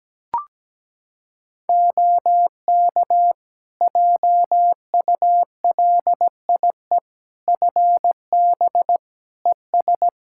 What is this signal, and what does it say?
Morse code, heard as a keyed tone, about 17 wpm, 700 Hz. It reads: OK JULIE FB ES
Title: Practice QSO - Exchange 2 - Info - 17wpm